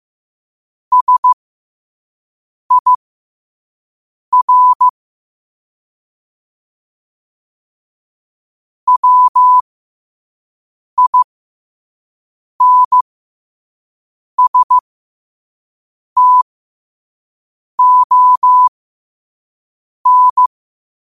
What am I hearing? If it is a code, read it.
Morse code, heard as a keyed tone, about 15 wpm, spaced as for 5 wpm, 1000 Hz. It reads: SIR WINSTON
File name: qotdfiles / 5wpm